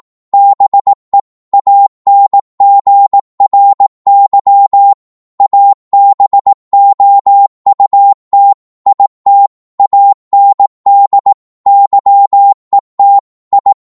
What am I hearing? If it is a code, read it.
Morse code, heard as a keyed tone, about 18 wpm, 800 Hz. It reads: BE ANGRY ABOUT IT AND YET I
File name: Morse / psg-18wpm